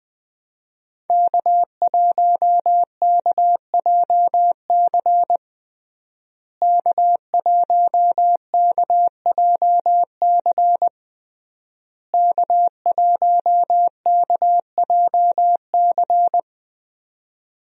Morse code, heard as a keyed tone, about 21 wpm, 700 Hz. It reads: K1KJC K1KJC K1KJC
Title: USA Call Signs - 5 Characters - Repeated 3x - 20wpm